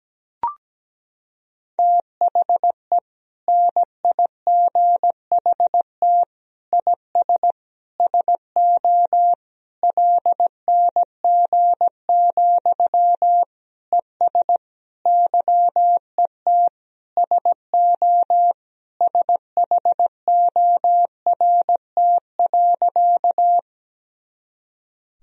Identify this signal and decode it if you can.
Morse code, heard as a keyed tone, about 17 wpm, 700 Hz. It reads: THE NIGHT IS SO LNG, ES YET SO SHORT.
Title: Sentences from Top 500 Words and Abbreviations 17wpm